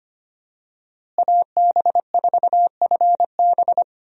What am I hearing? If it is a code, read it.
Morse code, heard as a keyed tone, about 25 wpm, 700 Hz. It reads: AB4FB
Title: USA Call Signs - 5 Characters - Repeated 3x - 25wpm